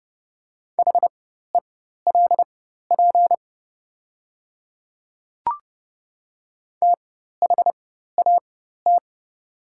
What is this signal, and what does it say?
Morse code, heard as a keyed tone, about 30 wpm, spaced as for 14 wpm, 700 Hz. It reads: HELP THAT
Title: 4x Character Spacing - Top 200 Words - 30wpm